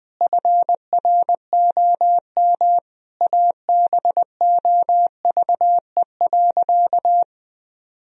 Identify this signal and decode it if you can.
Morse code, heard as a keyed tone, about 20 wpm, 700 Hz. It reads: FROM ABOVE.
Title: Sentences from Top 1000 Words - Review Part 2 - 20wpm